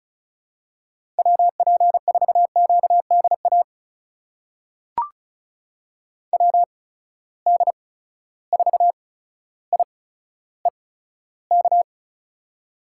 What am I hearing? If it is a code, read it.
Morse code, heard as a keyed tone, about 36 wpm, spaced as for 15 wpm, 700 Hz. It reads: WP4QDA WD4IEK
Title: USA Call Signs - 6 Characters - 8x Character Spacing - 35wpm